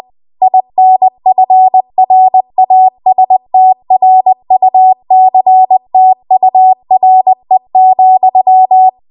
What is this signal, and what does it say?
Morse code, heard as a keyed tone, about 20 wpm, 750 Hz. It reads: INFRASTRUCTURE,
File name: Morse / 20WPM